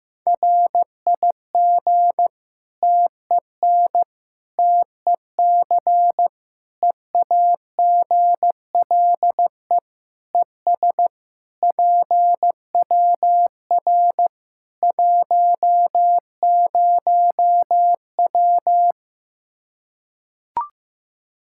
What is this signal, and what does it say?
Morse code, heard as a keyed tone, about 15 wpm, 700 Hz. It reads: RIG TEN TEC EAGLE ES PWR 10W
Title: K1USN SST QSO Practice 15wpm